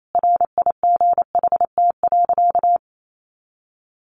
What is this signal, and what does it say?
Morse code, heard as a keyed tone, about 28 wpm, 700 Hz. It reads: RIGHT.
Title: Sentences from Top 300 Words 28wpm